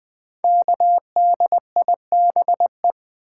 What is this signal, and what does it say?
Morse code, heard as a keyed tone, about 20 wpm, 700 Hz. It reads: KDIBE
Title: Code Groups - 5 Letters - 20wpm